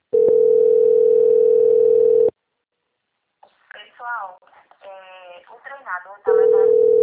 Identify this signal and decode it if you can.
Ringback tone (North America), 440 + 480 Hz, with speech.
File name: out- - - - - .72765